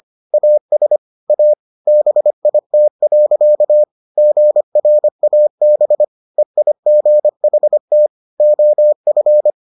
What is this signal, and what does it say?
Morse code, heard as a keyed tone, about 25 wpm, 600 Hz. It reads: AS A BIT. GRAB EIGHT OF